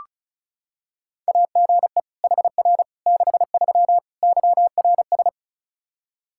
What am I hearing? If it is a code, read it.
Morse code, heard as a keyed tone, about 36 wpm, 700 Hz. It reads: AGE HR 63 YRS
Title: Practice QSO - Exchange 3 - More Info - 35wpm